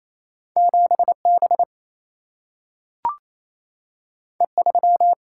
Morse code, heard as a keyed tone, about 28 wpm, 700 Hz. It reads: 7B E3
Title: Single-Letter Single-Number 28wpm